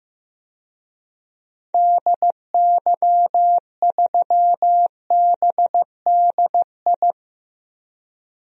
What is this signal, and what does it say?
Morse code, heard as a keyed tone, about 16 wpm, 700 Hz. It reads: DY3BDI
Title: DXCC-100 Call Signs - 6 Characters - Repeated 3x - 15wpm